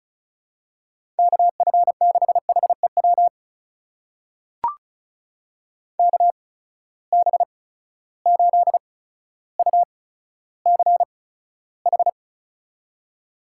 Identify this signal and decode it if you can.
Morse code, heard as a keyed tone, about 36 wpm, spaced as for 15 wpm, 700 Hz. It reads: KF6HEW KB8UCH
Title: USA Call Signs - 6 Characters - 8x Character Spacing - 35wpm